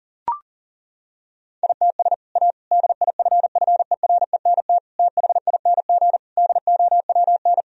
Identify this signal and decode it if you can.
Morse code, heard as a keyed tone, about 40 wpm, 700 Hz. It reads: ITS A DIFFERENT THING DOWN
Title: Sentences from Top 1000 Words - Review Part 3 - 40wpm